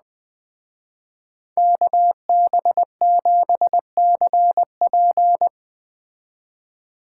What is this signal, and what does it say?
Morse code, heard as a keyed tone, about 20 wpm, 700 Hz. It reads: KB7CP
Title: Speed Racing - USA 2x2 Call Signs 17wpm